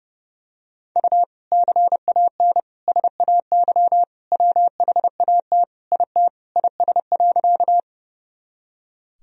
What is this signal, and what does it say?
Morse code, heard as a keyed tone, about 30 wpm, 700 Hz. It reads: U CAN SAY WHAT IT IS.
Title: Sentences from Top 200 Words and Abbreviations 30wpm